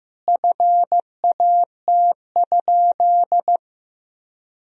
Morse code, heard as a keyed tone, about 16 wpm, 700 Hz. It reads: FAT?
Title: Sentences from Top 1000 Words - Review Part 4 - 15wpm